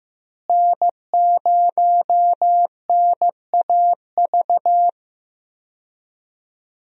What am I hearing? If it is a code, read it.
Morse code, heard as a keyed tone, about 15 wpm, 700 Hz. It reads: N0NAV
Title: USA Call Signs - 5 Characters - Repeated 3x - 15wpm